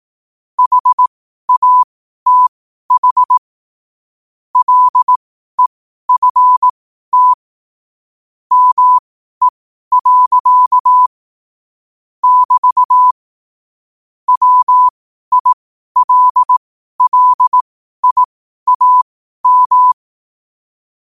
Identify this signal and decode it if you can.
Morse code, heard as a keyed tone, about 18 wpm, spaced as for 12 wpm, 1000 Hz. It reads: HATH LEFT ME. <BT> WILLIAM